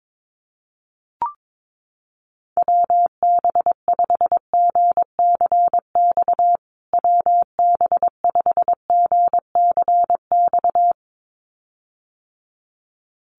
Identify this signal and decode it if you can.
Morse code, heard as a keyed tone, about 22 wpm, 700 Hz. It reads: WB5GCX WB5GCX
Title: USA Call Signs Repeated 2x 22wpm